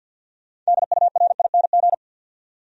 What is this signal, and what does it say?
Morse code, heard as a keyed tone, about 64 wpm, 700 Hz. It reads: DURING